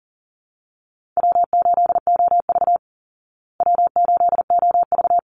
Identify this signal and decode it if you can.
Morse code, heard as a keyed tone, about 40 wpm, 700 Hz. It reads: W8OV W8OV
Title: Easy Call Signs Repeated 2x - 40wpm